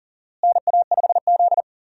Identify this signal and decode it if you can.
Morse code, heard as a keyed tone, about 40 wpm, 700 Hz. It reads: NA5Z
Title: USA Call Signs - 4 Characters - Repeated 2x - 40wpm